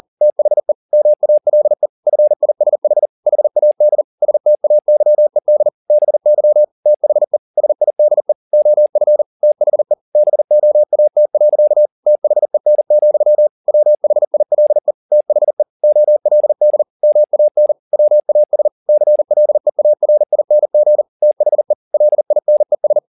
Morse code, heard as a keyed tone, about 40 wpm, 600 Hz. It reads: THE MALE FISH HAD STAYED BY THE SIDE OF THE BOAT. THEN, WHILE THE OLD MAN WAS CLEARING THE LINES